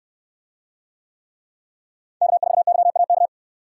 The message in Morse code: K4YAK

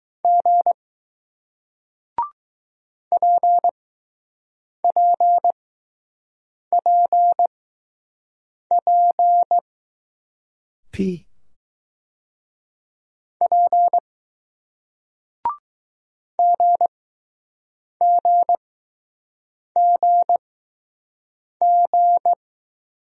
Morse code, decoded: G P P P P P G G G G

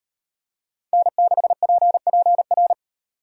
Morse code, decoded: N6PPR

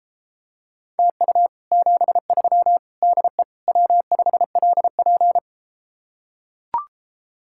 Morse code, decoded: TU 73 DE W5LP

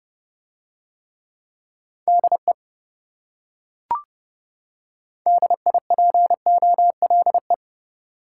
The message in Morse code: DE DIPOLE

